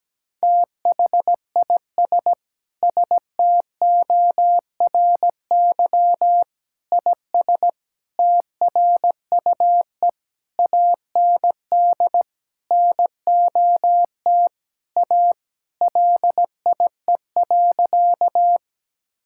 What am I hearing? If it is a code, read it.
Morse code, heard as a keyed tone, about 17 wpm, 700 Hz. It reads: THIS STORY IS TRUE AND NOT A LIE.
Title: Sentences from Top 1600 Words 17wpm